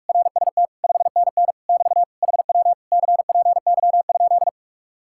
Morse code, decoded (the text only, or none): RST 5NN <BT> HW CPY?